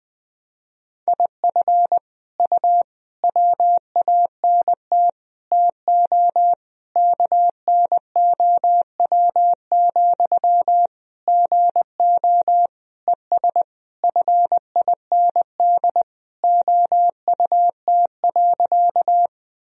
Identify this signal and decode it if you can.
Morse code, heard as a keyed tone, about 20 wpm, 700 Hz. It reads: IF U WANT TO KNOW, GO ES FIND OUT.
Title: Sentences from Top 200 Words and Abbreviations 20wpm